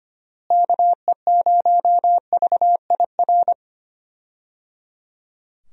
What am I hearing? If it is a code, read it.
Morse code, heard as a keyed tone, about 25 wpm, 700 Hz. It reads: KE0VIR
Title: USA Call Signs - 6 Characters - Repeated 3x - 25wpm